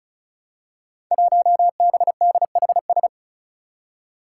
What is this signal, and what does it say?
Morse code, heard as a keyed tone, about 36 wpm, 700 Hz. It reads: 1BDHS
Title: Five Challenging Letter-Number Combinations 35wpm